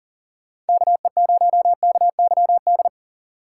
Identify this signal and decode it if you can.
Morse code, heard as a keyed tone, about 38 wpm, 700 Hz. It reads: KE0KYD
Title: Speed Racing - USA 2x3 Call Signs 40wpm